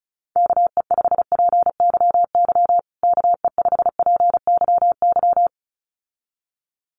Morse code, heard as a keyed tone, about 34 wpm, 700 Hz. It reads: KE5PYY KE5PYY
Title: USA Call Signs Repeated 2x 35wpm